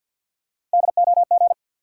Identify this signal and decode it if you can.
Morse code, heard as a keyed tone, about 52 wpm, 700 Hz. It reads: DOG
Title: Sentences from Top 400 Words - New Word Drill - 50wpm